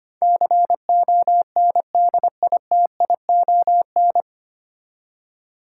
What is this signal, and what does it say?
Morse code, heard as a keyed tone, about 25 wpm, 700 Hz. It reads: CONDITION
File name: Nine Letter Words - 25wpm